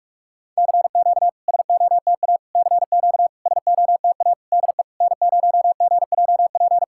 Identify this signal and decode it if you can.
Morse code, heard as a keyed tone, about 46 wpm, 690 Hz. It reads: CQ SOTA CQ SOTA DE N0GJP
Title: SOTA QSO Practice 45wpm